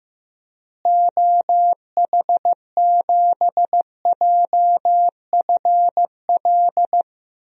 Morse code, heard as a keyed tone, about 15 wpm, 700 Hz. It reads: OH7JFL
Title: DXCC-100 Call Signs - 6 Characters - Repeated 3x - 15wpm